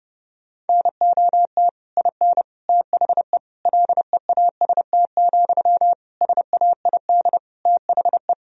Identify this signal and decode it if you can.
Morse code, heard as a keyed tone, about 29 wpm, 700 Hz. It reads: NOT IN THE LEAST, SAID THE
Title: Sentences from Top 1000 Words - Review Part 4 - 30wpm